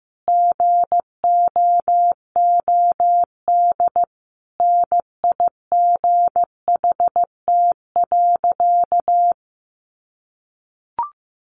Morse code, heard as a keyed tone, about 15 wpm, 700 Hz. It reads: GOOD NIGHT.
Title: Sentences from Top 300 Words 15wpm